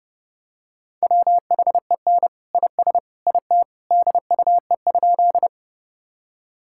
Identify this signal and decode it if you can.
Morse code, heard as a keyed tone, about 30 wpm, 700 Hz. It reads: WHEN IS IT DUE?
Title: Sentences from Top 1000 Words - Review Part 6 - 30wpm